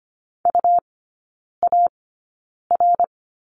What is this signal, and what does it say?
Morse code, heard as a keyed tone, about 25 wpm, spaced as for 9 wpm, 700 Hz. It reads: UAR
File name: Lesson 17e - Letter u - Three Characters 25wpm